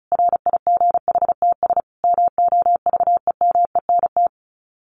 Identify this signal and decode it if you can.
Morse code, heard as a keyed tone, about 34 wpm, 700 Hz. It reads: RIGHTS MOVEMENT
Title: Sets of 5 Words Encore2 35wpm